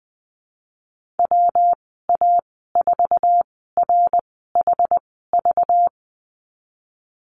Morse code, heard as a keyed tone, about 20 wpm, spaced as for 15 wpm, 700 Hz. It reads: WA4RHV
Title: 213 Lesson 34g - Number 8 - Call Signs 20wpm